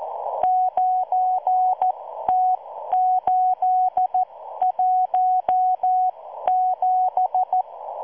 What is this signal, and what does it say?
Morse code, heard as a keyed tone, about 14 wpm, spaced as for 12 wpm, 740 Hz. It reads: 9T817